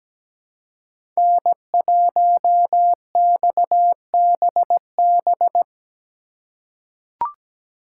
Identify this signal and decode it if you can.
Morse code, heard as a keyed tone, about 17 wpm, 700 Hz. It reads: N1XBB
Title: USA Call Signs - 5 Characters - 2x Character Spacing - 17wpm